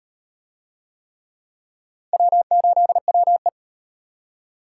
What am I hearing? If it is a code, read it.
Morse code, heard as a keyed tone, about 38 wpm, 700 Hz. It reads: W8WE